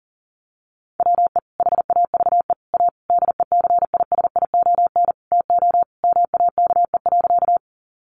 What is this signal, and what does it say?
Morse code, heard as a keyed tone, about 40 wpm, 700 Hz. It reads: WE HAVE A DECISION TO MAKE.
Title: Sentences from Top 1000 Words 40wpm